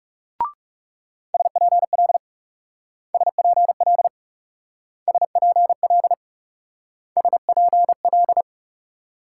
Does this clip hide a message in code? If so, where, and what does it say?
Yes: Morse code from 1.3 to 8.4 s, heard as a keyed tone, about 38 wpm, 700 Hz. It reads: SPL SPL SPL SPL